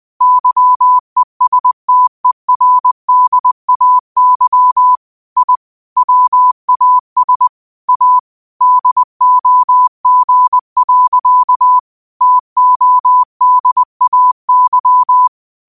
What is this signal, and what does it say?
Morse code, heard as a keyed tone, about 20 wpm, 1000 Hz. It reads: YESTERDAY I WAS A DOG. TODAY